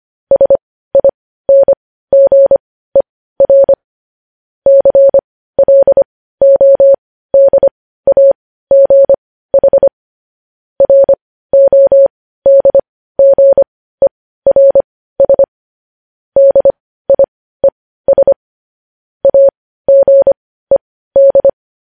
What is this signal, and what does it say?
Morse code, heard as a keyed tone, about 25 wpm, spaced as for 15 wpm, 550 Hz. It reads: SINGER CLODAGH RODGERS DIES AGED
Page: news-headlines-in-morse-code-at-15-wpm